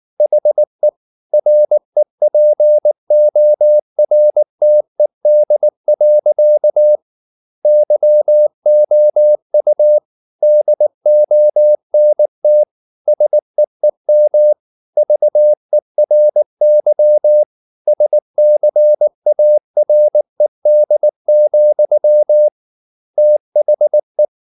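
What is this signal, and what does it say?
Morse code, heard as a keyed tone, about 19 wpm, 600 Hz. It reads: HE REPORTED. YOU DONT SEEM VERY SCARED, THE
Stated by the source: Morse code at 19 wpm from CW med Gnister, reading Ken Follett's novel Never in English.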